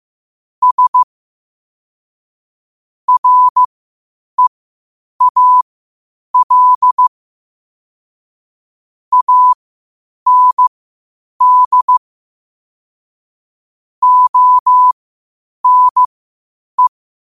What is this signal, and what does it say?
Morse code, heard as a keyed tone, about 15 wpm, spaced as for 8 wpm, 1000 Hz. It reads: S REAL AND ONE
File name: qotdfiles / 8wpm